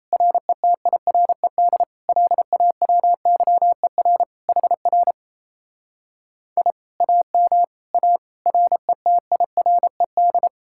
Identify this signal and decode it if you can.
Morse code, heard as a keyed tone, about 37 wpm, 700 Hz. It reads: RETIRED LAWYER HR I AM A RETIRED